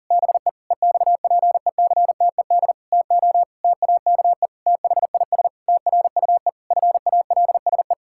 Morse code, heard as a keyed tone, about 40 wpm, 700 Hz. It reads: BE EXPECTED TO TAKE THIS TRUE FALSE
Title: Pangram Challenge v2 - 40wpm